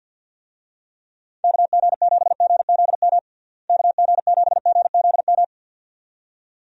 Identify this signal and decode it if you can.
Morse code, heard as a keyed tone, about 51 wpm, 700 Hz. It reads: KG7GZM KG7GZM
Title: USA Call Signs - 6 Characters - Repeated 2x - 50wpm